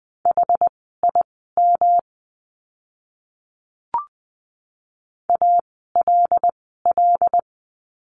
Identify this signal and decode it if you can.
Morse code, heard as a keyed tone, about 20 wpm, spaced as for 14 wpm, 700 Hz. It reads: HIM ALL
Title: 237 Lesson 38f - Letter q - Words 20wpm